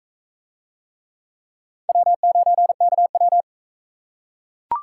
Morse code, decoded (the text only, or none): W9KW